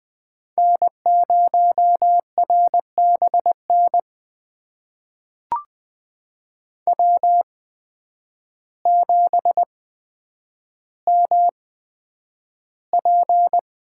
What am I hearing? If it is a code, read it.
Morse code, heard as a keyed tone, about 20 wpm, spaced as for 9 wpm, 700 Hz. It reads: N0RBN W7MP
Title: USA Call Signs - 5 Characters - 8x Character Spacing - 20wpm